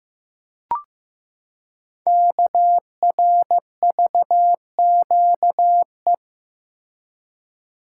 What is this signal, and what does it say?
Morse code, heard as a keyed tone, about 15 wpm, 700 Hz. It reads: KRVQE
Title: Code Groups - 5 Letters - 15wpm